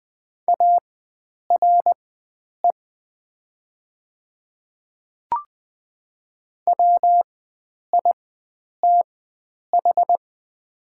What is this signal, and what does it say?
Morse code, heard as a keyed tone, about 20 wpm, spaced as for 9 wpm, 700 Hz. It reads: ARE WITH
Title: 4x Character Spacing - Top 100 Words - 20wpm